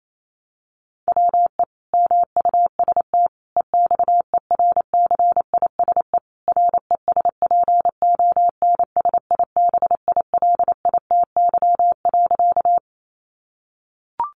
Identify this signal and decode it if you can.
Morse code, heard as a keyed tone, about 28 wpm, 700 Hz. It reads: WE MUST EXERCISE RESPONSIBILITY.
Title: Sentences from Top 1200 Words 28wpm